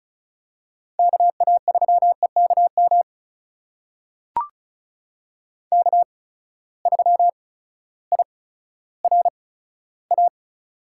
Morse code, heard as a keyed tone, about 36 wpm, spaced as for 16 wpm, 700 Hz. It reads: KA3EKM K3IRA
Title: USA Call Signs - 8x Character Spacing - 35wpm